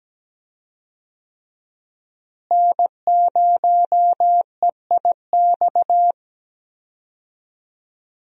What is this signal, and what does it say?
Morse code, heard as a keyed tone, about 17 wpm, 700 Hz. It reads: N0EIX
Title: USA Call Signs - 5 Characters - Repeated 3x - 17wpm